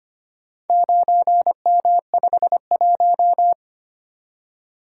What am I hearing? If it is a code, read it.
Morse code, heard as a keyed tone, about 25 wpm, 700 Hz. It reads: 9M51
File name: practice / 25wpm / Four Letter-Number Combinations - 25wpm